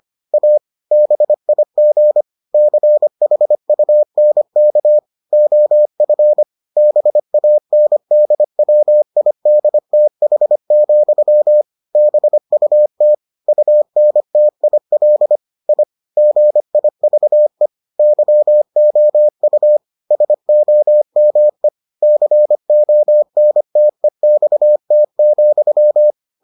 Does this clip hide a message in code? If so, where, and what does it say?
Yes: Morse code from 0.3 to 26.1 s, heard as a keyed tone, about 25 wpm, 600 Hz. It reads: A BIG CHUNK OF BANDWIDTH, BUT UNTIL I GIVE YOU SOME CONTEXT,